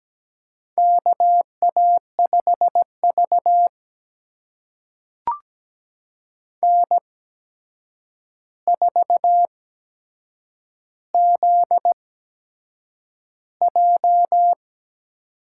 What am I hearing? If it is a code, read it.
Morse code, heard as a keyed tone, about 17 wpm, spaced as for 7 wpm, 700 Hz. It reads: KA5V N4ZJ
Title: USA Call Signs - 4 Characters - 8x Character Spacing - 17wpm